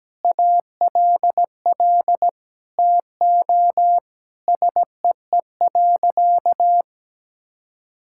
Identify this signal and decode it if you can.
Morse code, heard as a keyed tone, about 17 wpm, 700 Hz. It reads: ALL TO SEE.
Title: Sentences from Top 1000 Words - Review Part 5 - 17wpm